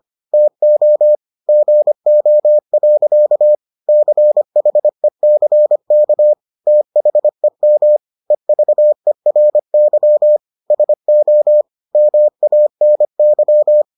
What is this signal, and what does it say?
Morse code, heard as a keyed tone, about 25 wpm, 600 Hz. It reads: TO GO. CHECK THEM EVERY SO MANY